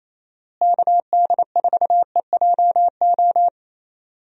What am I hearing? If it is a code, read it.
Morse code, heard as a keyed tone, about 28 wpm, 700 Hz. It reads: KD4EJO
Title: USA Call Signs - 6 Characters - Repeated 3x - 28wpm